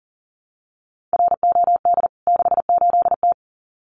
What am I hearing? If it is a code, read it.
Morse code, heard as a keyed tone, about 40 wpm, 700 Hz. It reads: ROD 68T
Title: CWT Exchange 40wpm